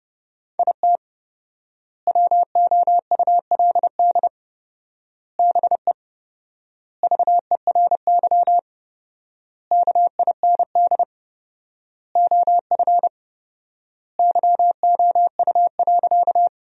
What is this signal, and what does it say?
Morse code, heard as a keyed tone, about 30 wpm, 700 Hz. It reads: IT WOULD BE VERY KIND OF YOU.
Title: 4x Word-Spacing - Sentences from Top 200 Words - 30wpm